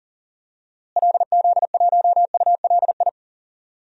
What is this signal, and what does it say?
Morse code, heard as a keyed tone, about 40 wpm, 700 Hz. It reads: LZ1ULI
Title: DXCC-100 Call Signs - 6 Characters - Repeated 2x - 40wpm